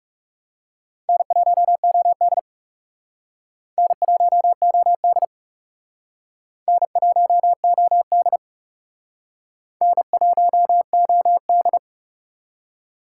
Morse code, heard as a keyed tone, about 38 wpm, 700 Hz. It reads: N1OD N1OD N1OD N1OD